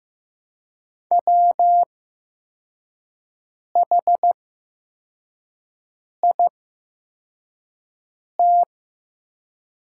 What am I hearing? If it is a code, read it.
Morse code, heard as a keyed tone, about 15 wpm, spaced as for 4 wpm, 700 Hz. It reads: WHIT
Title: 8x Character Spacing - Top 500 Words - 15wpm